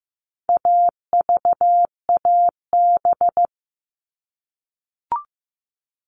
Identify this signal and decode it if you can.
Morse code, heard as a keyed tone, about 15 wpm, 700 Hz. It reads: AVAB
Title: All Cut Numbers - 4-Digit 15wpm